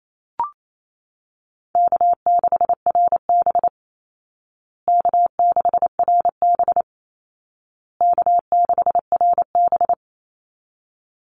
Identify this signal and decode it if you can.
Morse code, heard as a keyed tone, about 28 wpm, 700 Hz. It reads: K6RB K6RB K6RB
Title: Easy Call Signs Repeated 3x - 28wpm